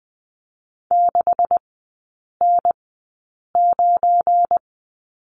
Morse code, heard as a keyed tone, about 20 wpm, spaced as for 8 wpm, 700 Hz. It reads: 6N9
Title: 198 Lesson 32e - Letter k - Three Characters 20wpm